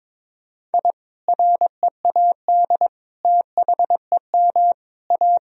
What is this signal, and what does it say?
Morse code, heard as a keyed tone, about 22 wpm, 700 Hz. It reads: I READ THEM A
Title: Sentences from Top 1000 Words - Review Part 1 - 22wpm